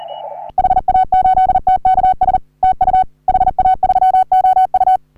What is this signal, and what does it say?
Morse code, heard as a keyed tone, about 38 wpm, 750 Hz. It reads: HA8TKS TU HA3OU
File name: CW / HA3OU RHOBCW REC